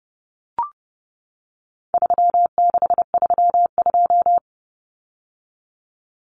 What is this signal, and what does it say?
Morse code, heard as a keyed tone, about 30 wpm, 700 Hz. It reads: 3632
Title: Common Cut Numbers - 4-Digit 30wpm